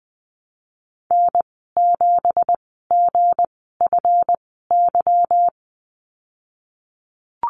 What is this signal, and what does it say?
Morse code, heard as a keyed tone, about 20 wpm, spaced as for 15 wpm, 700 Hz. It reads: N7GFY
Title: 179 Lesson 29g - Slash - Call Signs 20wpm